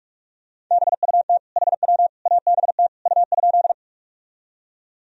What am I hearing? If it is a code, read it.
Morse code, heard as a keyed tone, about 46 wpm, 700 Hz. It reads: BUT HW ABT U?
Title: Sentences from Top 100 Words and Abbreviations 45wpm